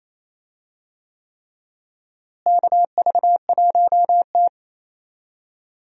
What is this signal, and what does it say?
Morse code, heard as a keyed tone, about 28 wpm, 700 Hz. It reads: KV1T
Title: USA Call Signs - 4 Characters - Repeated 3x - 28wpm